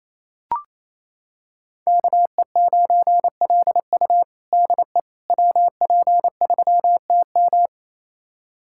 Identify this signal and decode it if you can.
Morse code, heard as a keyed tone, about 28 wpm, 700 Hz. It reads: KE9LU DE WP3TM
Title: USA Callsign DE Callsign - 5 Characters - 28wpm